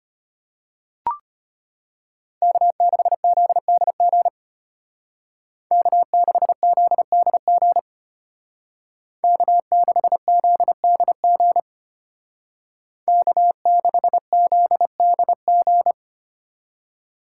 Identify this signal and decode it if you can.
Morse code, heard as a keyed tone, about 31 wpm, 700 Hz. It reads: K6ZDG K6ZDG K6ZDG K6ZDG
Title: Speed Racing - USA 1x3 Call Signs 25wpm